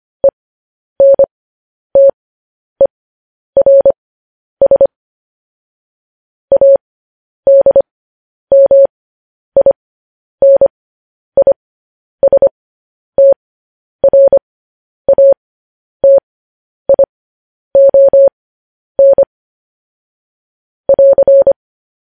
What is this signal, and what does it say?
Morse code, heard as a keyed tone, about 25 wpm, spaced as for 10 wpm, 550 Hz. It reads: ENTERS ADMINISTRATION <AR>
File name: podcast / News headlines-1x-25x10